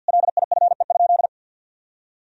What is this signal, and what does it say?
Morse code, heard as a keyed tone, about 52 wpm, 700 Hz. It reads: LIFE?
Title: Sentences from Top 300 Words and Abbreviations 50wpm